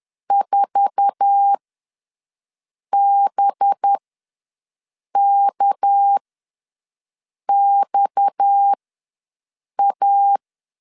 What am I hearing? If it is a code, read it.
Morse code, heard as a keyed tone, about 10 wpm, spaced as for 5 wpm, 800 Hz. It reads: 4BKXA
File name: LEZIONI / 9 286 ws 10wpm 4032